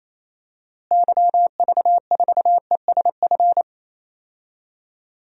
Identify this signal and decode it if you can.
Morse code, heard as a keyed tone, about 28 wpm, 700 Hz. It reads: YV4ESF
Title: DXCC-100 Call Signs - 6 Characters - Repeated 3x - 28wpm